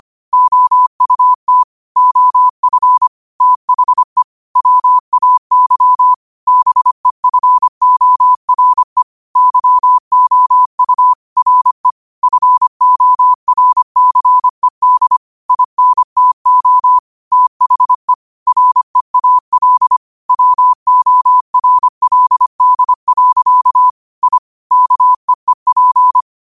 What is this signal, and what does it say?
Morse code, heard as a keyed tone, about 25 wpm, 1000 Hz. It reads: OUT OF THE WAY BEFORE YOU RE FORCED INTO THE REAL WORLD. I KEEP
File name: qotdfiles / 25wpm